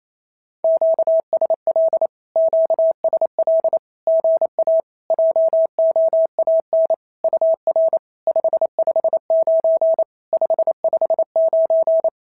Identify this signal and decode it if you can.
Morse code, heard as a keyed tone, about 28 wpm, 650 Hz. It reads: QSL QSL GA JOAN UR 559 559
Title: SOTA QSO Practice 28wpm